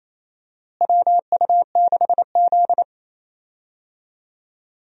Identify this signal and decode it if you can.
Morse code, heard as a keyed tone, about 28 wpm, 700 Hz. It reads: WU6Z
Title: USA Call Signs - 4 Characters - Repeated 2x - 28wpm